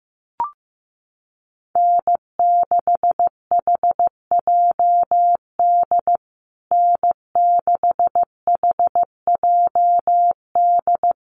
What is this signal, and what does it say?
Morse code, heard as a keyed tone, about 15 wpm, 700 Hz. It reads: N6HJD N6HJD
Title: USA Call Signs Repeated 3x 15wpm